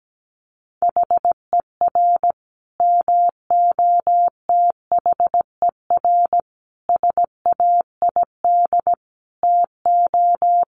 Morse code, heard as a keyed tone, about 17 wpm, 700 Hz. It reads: HER MOTHER SAID TO
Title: Sentences from Top 1300 Words 17wpm